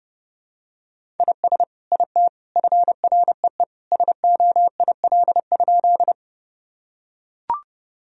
Morse code, heard as a keyed tone, about 30 wpm, 700 Hz. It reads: IS IT FREE SOIL?
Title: Sentences from Top 1000 Words - Review Part 4 - 30wpm